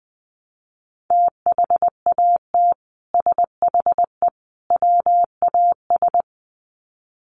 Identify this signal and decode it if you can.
Morse code, heard as a keyed tone, about 20 wpm, 700 Hz. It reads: THAT SHE WAS
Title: Sets of 3 Words 20wpm